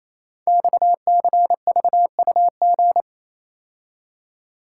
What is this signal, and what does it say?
Morse code, heard as a keyed tone, about 29 wpm, 690 Hz. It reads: XCVUG
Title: Code Groups - 5 Letter:Numbers - 28wpm